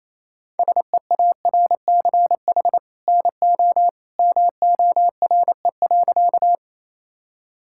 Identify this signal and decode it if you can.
Morse code, heard as a keyed tone, about 28 wpm, 700 Hz. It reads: SEARCH NO MORE.
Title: Sentences from Top 1000 Words - Review Part 5 - 28wpm